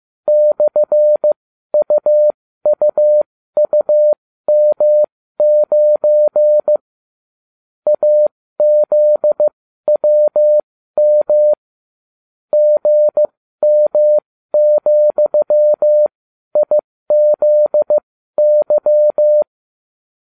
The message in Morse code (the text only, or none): /UUUM9 AZWM GM,IZY